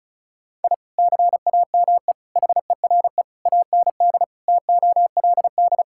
Morse code, heard as a keyed tone, about 36 wpm, 700 Hz. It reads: I CAME HERE AND TOLD
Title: Sentences from Top 1000 Words - Review Part 6 - 35wpm